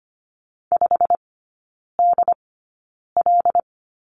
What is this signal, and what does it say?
Morse code, heard as a keyed tone, about 25 wpm, spaced as for 9 wpm, 700 Hz. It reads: 5DL
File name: Lesson 17e - Letter u - Three Characters 25wpm